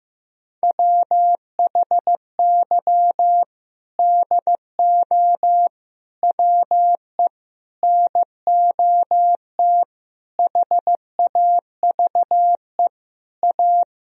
Morse code, heard as a keyed tone, about 15 wpm, 700 Hz. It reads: WHY DO WE NOT HAVE A
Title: Sentences from Top 1000 Words - Review Part 3 - 15wpm